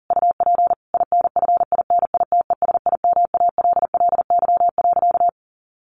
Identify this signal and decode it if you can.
Morse code, heard as a keyed tone, about 40 wpm, 700 Hz. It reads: UP INFINITESIMALLY.
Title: A Sound Of Thunder 40wpm